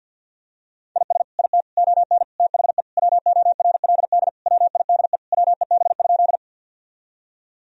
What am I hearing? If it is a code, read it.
Morse code, heard as a keyed tone, about 50 wpm, 700 Hz. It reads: IS IT ON THE WORLD WIDE WEB?
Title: Sentences from Top 1000 Words - Review Part 4 - 50wpm